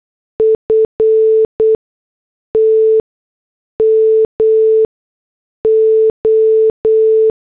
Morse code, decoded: FT MO